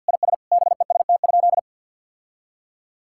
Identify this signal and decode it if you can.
Morse code, heard as a keyed tone, about 49 wpm, 700 Hz. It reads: IS BEST?
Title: Sentences from Top 1000 Words - Review Part 5 - 50wpm